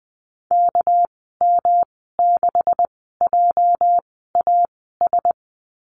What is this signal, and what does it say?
Morse code, heard as a keyed tone, about 20 wpm, spaced as for 14 wpm, 700 Hz. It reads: KM6JAS
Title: 207 Lesson 33g - Letter j - Call Signs 20wpm